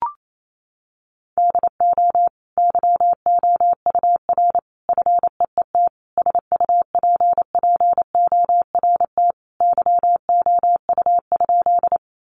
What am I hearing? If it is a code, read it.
Morse code, heard as a keyed tone, about 28 wpm, 700 Hz. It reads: DO YOUR FEET SUPPORT YOU?
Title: Sentences from Top 800 Words 28wpm